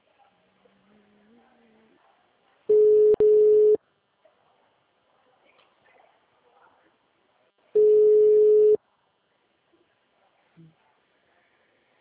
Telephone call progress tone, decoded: ringback tone

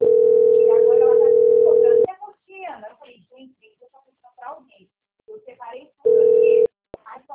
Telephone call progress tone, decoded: ringback tone

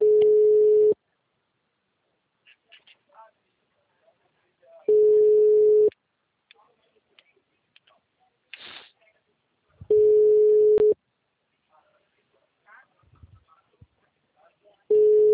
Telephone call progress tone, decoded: ringback tone

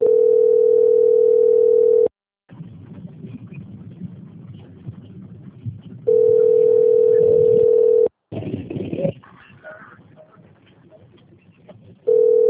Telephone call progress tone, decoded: ringback tone

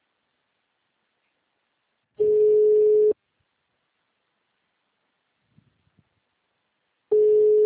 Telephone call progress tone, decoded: ringback tone

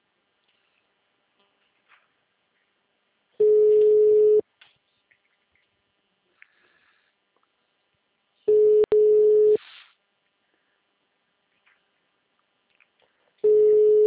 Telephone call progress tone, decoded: ringback tone

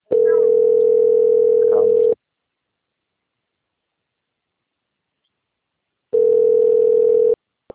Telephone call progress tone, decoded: ringback tone

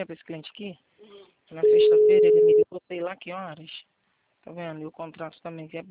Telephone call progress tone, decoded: ringback tone

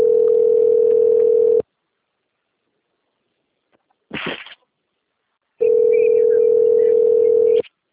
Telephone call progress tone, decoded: ringback tone

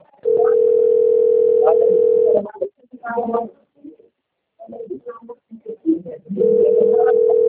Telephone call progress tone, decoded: ringback tone

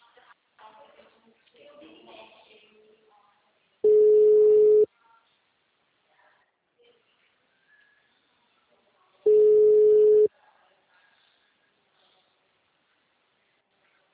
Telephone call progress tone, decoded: ringback tone

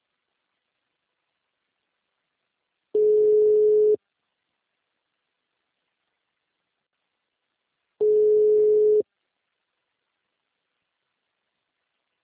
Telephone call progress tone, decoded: ringback tone

